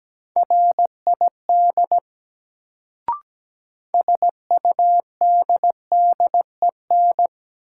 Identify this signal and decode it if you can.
Morse code, heard as a keyed tone, about 17 wpm, 700 Hz. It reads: RID SUDDEN
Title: Sets of 4 Words - New Word Drill - 17wpm